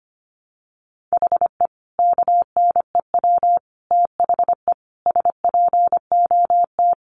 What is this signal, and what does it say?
Morse code, heard as a keyed tone, about 25 wpm, 700 Hz. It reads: HE KNEW THE SPOT
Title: Sentences from Top 1100 Words 25wpm